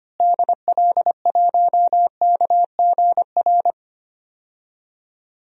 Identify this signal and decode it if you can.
Morse code, heard as a keyed tone, about 25 wpm, 700 Hz. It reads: DL1KGR
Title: DXCC-100 Call Signs - 6 Characters - Repeated 3x - 25wpm